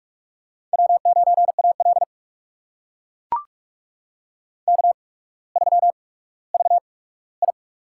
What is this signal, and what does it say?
Morse code, heard as a keyed tone, about 44 wpm, spaced as for 19 wpm, 700 Hz. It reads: W9AL K3VI
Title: USA Call Signs - 4 Characters - 8x Character Spacing - 45wpm